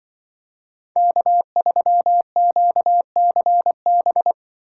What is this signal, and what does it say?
Morse code, heard as a keyed tone, about 25 wpm, 700 Hz. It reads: K3QCB
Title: Speed Racing - USA Call Signs 20wpm